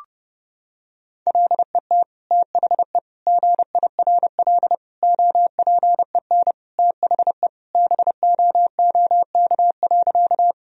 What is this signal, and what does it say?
Morse code, heard as a keyed tone, about 32 wpm, 700 Hz. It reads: LET THE GIRL OPEN THE BOOK.
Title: Sentences from Top 1000 Words - Review Part 1 - 30wpm